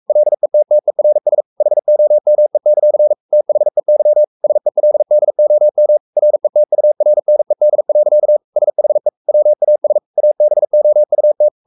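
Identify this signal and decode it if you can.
Morse code, heard as a keyed tone, about 44 wpm, 600 Hz. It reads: LETTERS HOME; THEY SELDOM RETURNED. SHE WAS ABOUT
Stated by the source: Morse code at 43 wpm sending Ken Follett's novel Never in English, from CW med Gnister.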